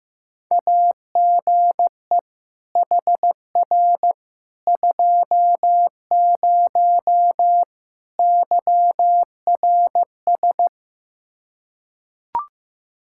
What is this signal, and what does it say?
Morse code, heard as a keyed tone, about 15 wpm, 700 Hz. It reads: AGE HR 20 YRS
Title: Practice QSO - Exchange 3 - More Info - 15wpm